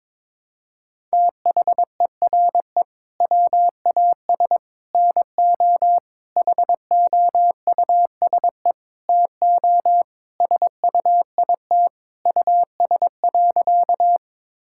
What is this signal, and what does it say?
Morse code, heard as a keyed tone, about 22 wpm, 700 Hz. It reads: THERE WAS NO HOUSE TO SUIT US.
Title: Sentences from Top 1600 Words 22wpm